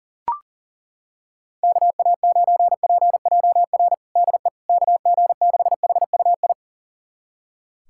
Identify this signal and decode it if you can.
Morse code, heard as a keyed tone, about 40 wpm, 700 Hz. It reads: KA9PJR DE KG6HUI
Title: USA Callsign DE Callsign 40wpm